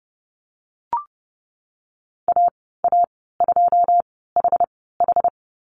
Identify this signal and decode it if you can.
Morse code, heard as a keyed tone, about 30 wpm, spaced as for 17 wpm, 700 Hz. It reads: AA2HH
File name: Lesson 14g - Number 2 - Call Signs 30wpm